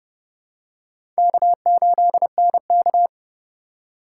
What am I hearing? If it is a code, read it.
Morse code, heard as a keyed tone, about 30 wpm, 700 Hz. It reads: K8NK
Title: USA Call Signs - 4x Character Spacing - 30wpm